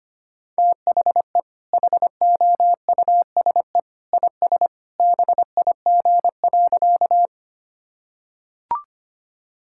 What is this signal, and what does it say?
Morse code, heard as a keyed tone, about 25 wpm, 700 Hz. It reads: THE HOUSE IS BIG.
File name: practice / 25wpm / Sentences with 2-Word Phrases - 25wpm